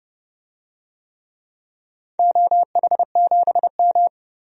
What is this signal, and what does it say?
Morse code, heard as a keyed tone, about 30 wpm, 700 Hz. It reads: OH7M
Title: DXCC-100 Call Signs - 4 Characters - 30wpm